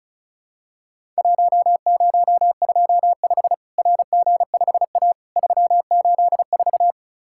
Morse code, heard as a keyed tone, about 36 wpm, 700 Hz. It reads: 1025 RG5A 384
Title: Worked All Europe DX Contest - QTC Exchange - 35wpm